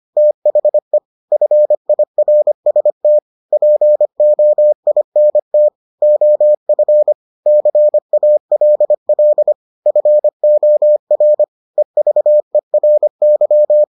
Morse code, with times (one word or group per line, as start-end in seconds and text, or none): 0.2-1.0 THE
1.3-3.2 FIRST
3.5-5.7 POINT
6.0-7.1 OF
7.5-9.5 CALL
9.9-11.4 FOR
11.8-13.8 EVERY